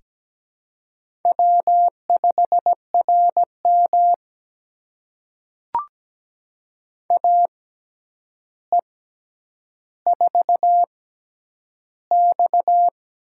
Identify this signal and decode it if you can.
Morse code, heard as a keyed tone, about 17 wpm, spaced as for 9 wpm, 700 Hz. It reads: W5RM AE4X
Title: USA Call Signs - 4 Characters - 6x Character Spacing - 17wpm